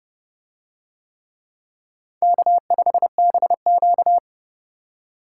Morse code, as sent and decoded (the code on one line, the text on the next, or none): -.- ..... -... --.-
K5BQ